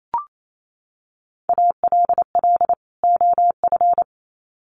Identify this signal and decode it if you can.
Morse code, heard as a keyed tone, about 28 wpm, 700 Hz. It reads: ALL OF
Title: Sets of 2 Words 28wpm